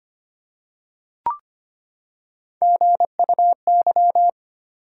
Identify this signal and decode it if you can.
Morse code, heard as a keyed tone, about 25 wpm, 700 Hz. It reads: GUY
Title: Three Letter Words - 25wpm